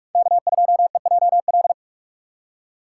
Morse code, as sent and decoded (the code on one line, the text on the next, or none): -.- ..--- . .--- .-..
K2EJL